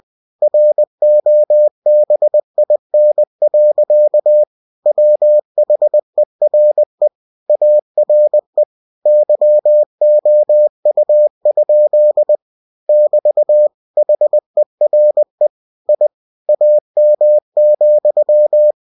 Morse code, heard as a keyed tone, about 20 wpm, 600 Hz. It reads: ROBIN. WHERE ARE YOU? <BT> HERE I AM,